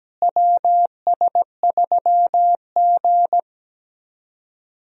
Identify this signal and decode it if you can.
Morse code, heard as a keyed tone, about 17 wpm, 700 Hz. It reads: WS3G
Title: USA Call Signs - 4 Characters - Repeated 2x - 17wpm